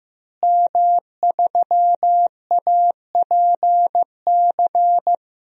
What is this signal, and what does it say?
Morse code, heard as a keyed tone, about 15 wpm, 700 Hz. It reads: M3APC
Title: DXCC-100 Call Signs - 5 Characters - Repeated 3x - 15wpm